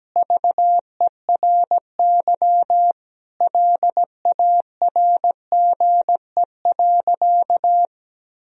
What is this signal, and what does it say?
Morse code, heard as a keyed tone, about 17 wpm, 700 Hz. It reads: VERY LARGE.
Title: Sentences from Top 1000 Words - Review Part 1 - 17wpm